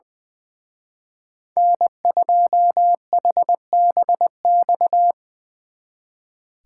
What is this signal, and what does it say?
Morse code, heard as a keyed tone, about 23 wpm, 700 Hz. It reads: N2HBX
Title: Speed Racing - USA 1x3 Call Signs 20wpm